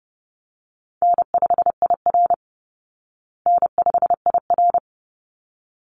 Morse code, heard as a keyed tone, about 28 wpm, 700 Hz. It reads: N5IR N5IR
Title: Call Signs Repeated 3x - 30wpm